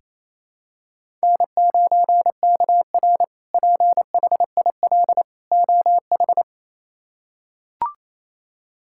Morse code, heard as a keyed tone, about 28 wpm, 700 Hz. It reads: N9KR PHIL OH
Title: K1USN SST QSO Practice - 28wpm